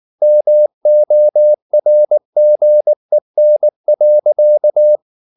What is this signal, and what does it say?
Morse code, heard as a keyed tone, about 19 wpm, 600 Hz. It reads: MORGEN.
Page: Ferie 19 wpm